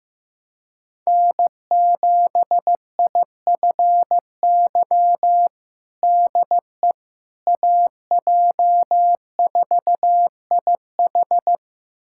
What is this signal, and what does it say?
Morse code, heard as a keyed tone, about 15 wpm, 700 Hz. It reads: N7IFY DE AJ4IH
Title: USA Callsign DE Callsign - 5 Characters - 15wpm